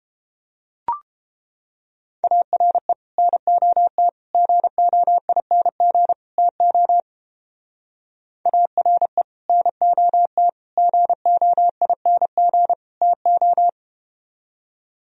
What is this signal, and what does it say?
Morse code, heard as a keyed tone, about 31 wpm, 700 Hz. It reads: ARE NOT GOING TO ARE NOT GOING TO